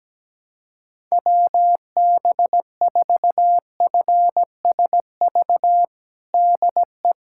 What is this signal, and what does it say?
Morse code, heard as a keyed tone, about 17 wpm, 700 Hz. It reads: WB4FSV DE
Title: USA Callsign DE Callsign - 6 Characters - 17wpm